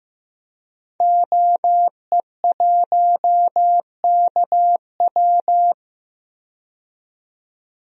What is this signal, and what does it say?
Morse code, heard as a keyed tone, about 15 wpm, 700 Hz. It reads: OE1KW
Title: DXCC-100 Call Signs - 5 Characters - 15wpm